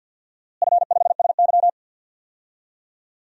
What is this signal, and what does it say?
Morse code, heard as a keyed tone, about 50 wpm, 700 Hz. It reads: F5SY
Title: DXCC-100 Call Signs - 4 Characters - Repeated 3x - 50wpm